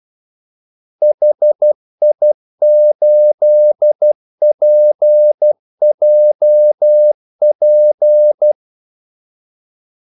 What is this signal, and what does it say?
Morse code, heard as a keyed tone, about 12 wpm, 600 Hz. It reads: HI8PJP
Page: 2 ekstra ordmellemrum mellem kaldesignalerne